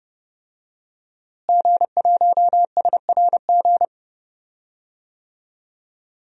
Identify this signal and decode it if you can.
Morse code, heard as a keyed tone, about 30 wpm, 700 Hz. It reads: G1SRG
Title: DXCC-100 Call Signs - 5 Characters - Repeated 2x - 30wpm